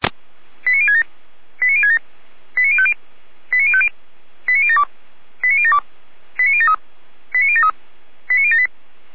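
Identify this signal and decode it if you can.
Five-tone selective call (ZVEI), heard as quick runs of five tones, tones of about 70 ms. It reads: 89067 89067 89050 89050 89072 89072 89073 89073 89078
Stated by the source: Five-tone ZVEI selective call 89022 received by SDR, archived by Feuerwehr Orlamünde (fire brigade).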